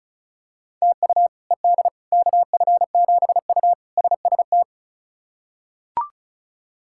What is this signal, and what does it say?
Morse code, heard as a keyed tone, about 36 wpm, 700 Hz. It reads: TU ED KF7U SST
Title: K1USN SST QSO Practice - 35wpm